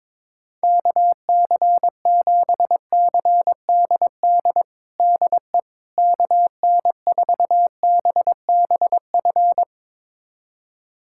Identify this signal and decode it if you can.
Morse code, heard as a keyed tone, about 22 wpm, 700 Hz. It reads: KC7CDD DE KN4BBF
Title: USA Callsign DE Callsign - 6 Characters - 22wpm